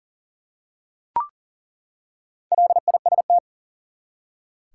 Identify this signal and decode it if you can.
Morse code, heard as a keyed tone, about 40 wpm, 700 Hz. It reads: LIST
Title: Four Letter Words - 40wpm